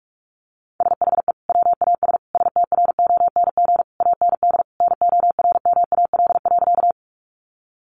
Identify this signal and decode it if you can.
Morse code, heard as a keyed tone, about 44 wpm, 700 Hz. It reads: SHE WAS STRONG AND NORMAL.
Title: Sentences from Top 1200 Words 45wpm